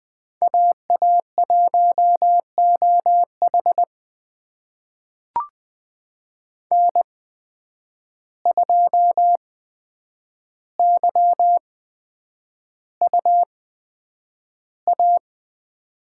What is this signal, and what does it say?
Morse code, heard as a keyed tone, about 20 wpm, spaced as for 9 wpm, 700 Hz. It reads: AA1OH N2YUA
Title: USA Call Signs - 5 Characters - 8x Character Spacing - 20wpm